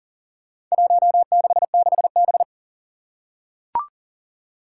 Morse code, heard as a keyed tone, about 40 wpm, 700 Hz. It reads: 166B
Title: Four Challenging Letter-Number Combinations 40wpm